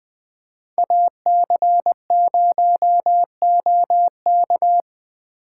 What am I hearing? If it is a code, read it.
Morse code, heard as a keyed tone, about 20 wpm, 700 Hz. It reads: AC0OK
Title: Speed Racing - USA Call Signs 17wpm